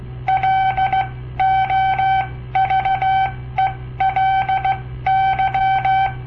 Morse code, heard as a keyed tone, about 16 wpm, spaced as for 13 wpm, 770 Hz. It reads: LOVELY
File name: papertapes / local / Code Practice-01